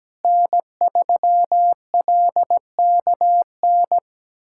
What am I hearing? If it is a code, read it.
Morse code, heard as a keyed tone, about 17 wpm, 700 Hz. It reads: N3LKN
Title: USA Callsign DE Callsign - 5 Characters - 17wpm